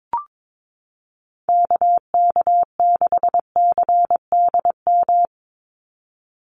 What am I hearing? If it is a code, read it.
Morse code, heard as a keyed tone, about 22 wpm, 700 Hz. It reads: KK6CDM
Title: USA Call Signs 22wpm